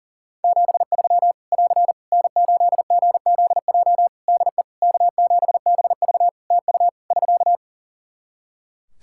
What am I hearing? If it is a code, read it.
Morse code, heard as a keyed tone, about 40 wpm, 700 Hz. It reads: 73 <AR> N8GZJ DE K7BV TU <SK>
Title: Practice QSO - Exchange 4 - Ending - 40wpm